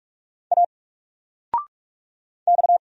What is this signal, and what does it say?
Morse code, heard as a keyed tone, about 46 wpm, 700 Hz. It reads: A X
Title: Single Character (Letter:Number:Punctuation:Prosign) 45wpm